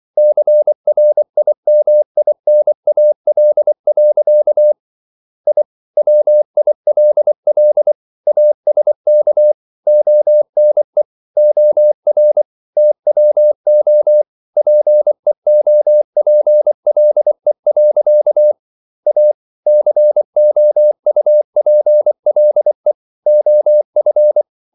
Morse code, heard as a keyed tone, about 24 wpm, 600 Hz. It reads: CRIMINAL. I WILL ASK ONE OR TWO PEOPLE. A COUPLE OF